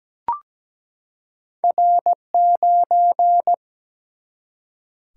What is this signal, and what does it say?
Morse code, heard as a keyed tone, about 17 wpm, 700 Hz. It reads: R9
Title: Single-Letter Single-Number 17wpm